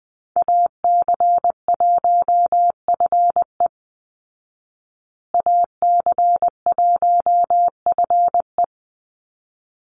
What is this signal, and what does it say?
Morse code, heard as a keyed tone, about 20 wpm, 700 Hz. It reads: AC1FE AC1FE
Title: Call Signs Repeated 3x - 20wpm